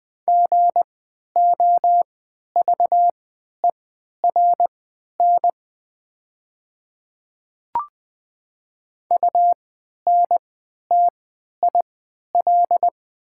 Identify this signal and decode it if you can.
Morse code, heard as a keyed tone, about 20 wpm, spaced as for 11 wpm, 700 Hz. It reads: GOVERN UNTIL
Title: 3x Character Spacing - Top 500 Words - 20wpm